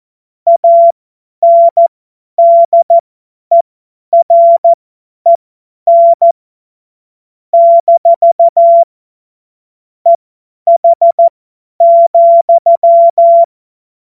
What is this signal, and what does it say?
Morse code, heard as a keyed tone, about 14 wpm, spaced as for 10 wpm, 680 Hz. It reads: ANDEREN - EH,
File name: Morsekurs / Witze / Witz 28. Dezember 2021, 50 F70